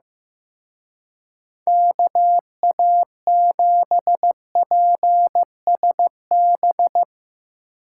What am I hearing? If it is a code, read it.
Morse code, heard as a keyed tone, about 17 wpm, 700 Hz. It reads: KA7PSB
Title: Speed Racing - USA Call Signs 15wpm